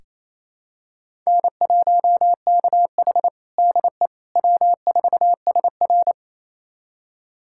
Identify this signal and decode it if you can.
Morse code, heard as a keyed tone, about 28 wpm, 700 Hz. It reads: N1KH DE W4SR
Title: USA Callsign DE Callsign - 4 Characters - 28wpm